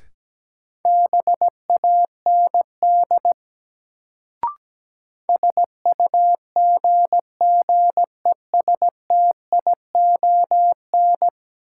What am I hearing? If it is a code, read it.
Morse code, heard as a keyed tone, about 17 wpm, 700 Hz. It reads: BAND SUGGESTION
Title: Sentences from Top 1600 Words - New Word Drill - 17wpm